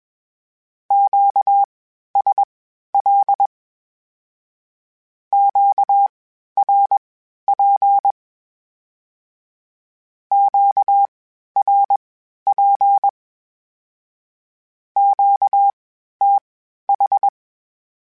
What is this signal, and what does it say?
Morse code, heard as a keyed tone, about 21 wpm, spaced as for 12 wpm, 800 Hz. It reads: QSL QRP QRP QTH
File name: (spaced) / Q-Calls Spaced-mp3s / Q Calls Twice sp @20